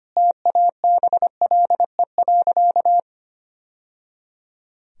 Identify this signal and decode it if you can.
Morse code, heard as a keyed tone, about 25 wpm, 700 Hz. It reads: TABLE.
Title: Sentences from Top 1000 Words - Review Part 2 - 25wpm